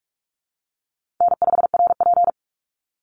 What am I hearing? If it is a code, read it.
Morse code, heard as a keyed tone, about 44 wpm, 700 Hz. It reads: N5RP
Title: Easy Call Signs Repeated 2x - 45wpm